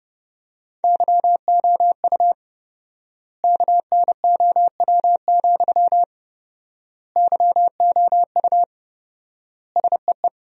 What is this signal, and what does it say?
Morse code, heard as a keyed tone, about 30 wpm, 700 Hz. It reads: YOU KNOW, YOU SEE
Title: 4x Word-Spacing - Sentences from Top 200 Words - 30wpm